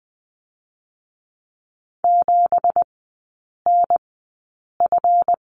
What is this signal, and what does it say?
Morse code, heard as a keyed tone, about 20 wpm, spaced as for 8 wpm, 690 Hz. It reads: 7NF
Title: 172 Lesson 28e - Number 9 - Three Characters 20wpm